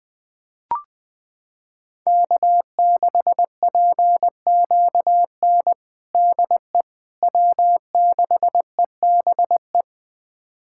Morse code, heard as a keyed tone, about 20 wpm, 700 Hz. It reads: K6PQN DE W6EBE
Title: USA Callsign DE Callsign - 5 Characters - 20wpm